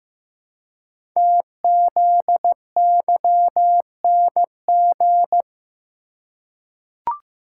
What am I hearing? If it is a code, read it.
Morse code, heard as a keyed tone, about 15 wpm, 700 Hz. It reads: TZYNG
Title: Code Groups - 5 Letter:Numbers - 15wpm